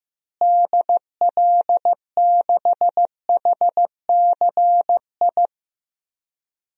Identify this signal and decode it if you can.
Morse code, heard as a keyed tone, about 15 wpm, 700 Hz. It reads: DL6HCI
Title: DXCC-100 Call Signs - 6 Characters - Repeated 3x - 15wpm